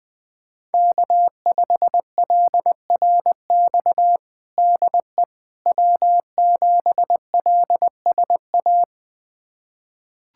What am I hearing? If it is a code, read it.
Morse code, heard as a keyed tone, about 20 wpm, 700 Hz. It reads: K5LRX DE W7LSA
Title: USA Callsign DE Callsign 20wpm